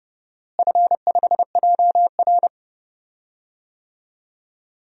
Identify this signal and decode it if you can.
Morse code, heard as a keyed tone, about 32 wpm, 700 Hz. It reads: F5JR
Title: DXCC-100 Call Signs - 4 Characters - Repeated 3x - 30wpm